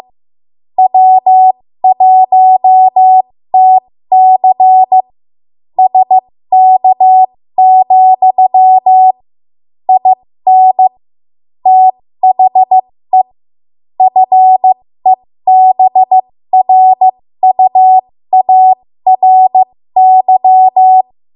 Morse code, heard as a keyed tone, about 15 wpm, spaced as for 13 wpm, 750 Hz. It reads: W1TC SK, IN THE FEBRUARY
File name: Morse / 13WPM